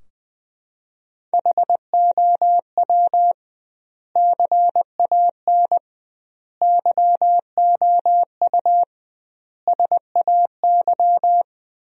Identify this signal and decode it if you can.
Morse code, heard as a keyed tone, about 21 wpm, 700 Hz. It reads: HOW CAN YOU SAY
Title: 2x Word-Spacing - Sentences from Top 200 Words - 20wpm